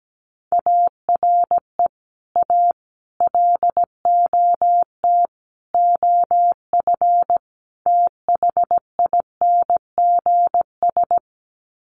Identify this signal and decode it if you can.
Morse code, heard as a keyed tone, about 17 wpm, 700 Hz. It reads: ARE A LOT OF THINGS
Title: Sets of 5 Words Encore2 17wpm